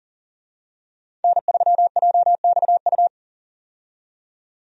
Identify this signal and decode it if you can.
Morse code, heard as a keyed tone, about 39 wpm, 700 Hz. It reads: N3JXU
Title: USA Call Signs - 5 Characters - 40wpm